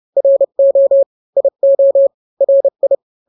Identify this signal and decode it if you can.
Morse code, heard as a keyed tone, about 30 wpm, 550 Hz. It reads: RO IO RI
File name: n-grams / 2-grams 30 25 WPM